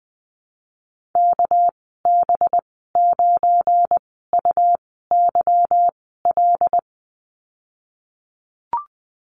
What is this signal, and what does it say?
Morse code, heard as a keyed tone, about 20 wpm, spaced as for 15 wpm, 700 Hz. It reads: KB9UYL
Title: 238 Lesson 38g - Letter q - Call Signs 20wpm